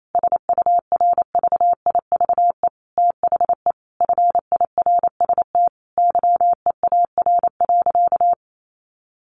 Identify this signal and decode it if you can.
Morse code, heard as a keyed tone, about 28 wpm, 700 Hz. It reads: SURVIVE THE FIRST YEAR.
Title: Sentences from Top 1300 Words 28wpm